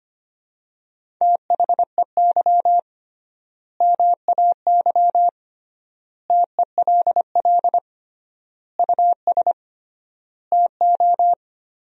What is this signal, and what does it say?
Morse code, heard as a keyed tone, about 25 wpm, 700 Hz. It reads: THEY MAY TELL US TO
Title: 3x Word-Spacing - Sentences from Top 300 Words - 25wpm